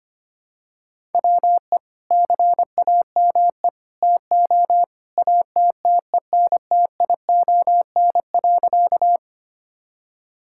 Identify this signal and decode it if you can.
Morse code, heard as a keyed tone, about 25 wpm, 700 Hz. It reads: WE CAME TO ATTENTION.
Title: Sentences from Top 1000 Words - Review Part 6 - 25wpm